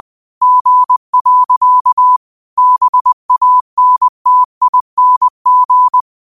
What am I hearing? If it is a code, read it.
Morse code, heard as a keyed tone, about 20 wpm, 1000 Hz. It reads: G. BANTING
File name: qotdfiles / 20wpm